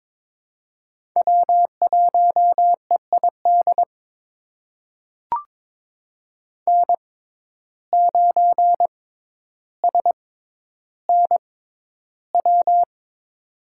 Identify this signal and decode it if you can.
Morse code, heard as a keyed tone, about 22 wpm, spaced as for 11 wpm, 700 Hz. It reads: W1EID N9SNW
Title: USA Call Signs - 5 Characters - 6x Character Spacing - 22wpm